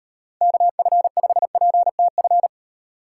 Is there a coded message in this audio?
Yes: Morse code from 0.4 to 2.5 s, heard as a keyed tone, about 35 wpm, 700 Hz. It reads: KF5PTF